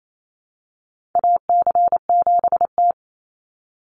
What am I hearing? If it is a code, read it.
Morse code, heard as a keyed tone, about 28 wpm, 700 Hz. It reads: AC7T
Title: Call Signs - 28wpm